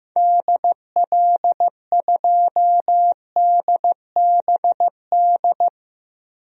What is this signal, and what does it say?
Morse code, heard as a keyed tone, about 15 wpm, 700 Hz. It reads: DL2DBD
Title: DXCC-100 Call Signs - 6 Characters - Repeated 2x - 15wpm